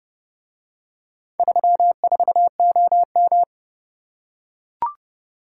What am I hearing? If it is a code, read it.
Morse code, heard as a keyed tone, about 29 wpm, 700 Hz. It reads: 34OM